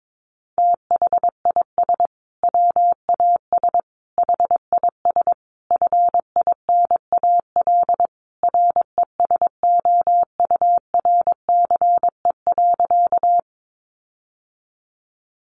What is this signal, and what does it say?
Morse code, heard as a keyed tone, about 22 wpm, 700 Hz. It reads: THIS WAS HIS FINAL RESOURCE.
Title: Sentences from Top 1300 Words 22wpm